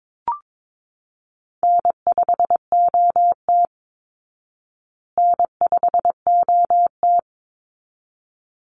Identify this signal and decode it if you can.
Morse code, heard as a keyed tone, about 22 wpm, 700 Hz. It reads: N5OT N5OT
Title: Call Signs Repeated 3x - 22wpm